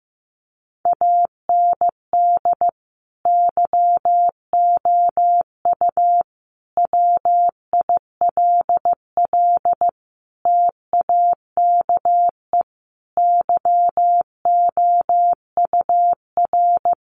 Morse code, heard as a keyed tone, about 15 wpm, 700 Hz. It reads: AND YOU WILL TAKE YOUR
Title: Sentences from Top 900 Words 15wpm